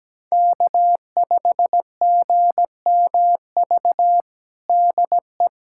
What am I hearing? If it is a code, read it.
Morse code, heard as a keyed tone, about 17 wpm, 700 Hz. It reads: K5GMV DE
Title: USA Callsign DE Callsign - 5 Characters - 17wpm